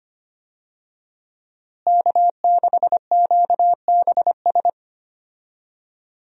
Morse code, heard as a keyed tone, about 25 wpm, 700 Hz. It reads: K6QBS
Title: USA Call Signs - 5 Characters - Repeated 2x - 25wpm